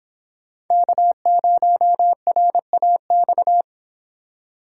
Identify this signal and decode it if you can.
Morse code, heard as a keyed tone, about 26 wpm, 700 Hz. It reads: K0RAX
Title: Speed Racing - USA 1x3 Call Signs 22wpm